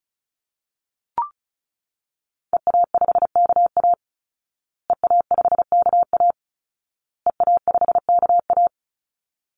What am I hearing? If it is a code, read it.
Morse code, heard as a keyed tone, about 34 wpm, 700 Hz. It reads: EA5KA EA5KA EA5KA
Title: Call Signs Repeated 2x - 35wpm